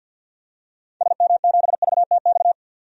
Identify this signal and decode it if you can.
Morse code, heard as a keyed tone, about 45 wpm, 700 Hz. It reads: SM7VTX
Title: DXCC-100 Call Signs - 6 Characters - 50wpm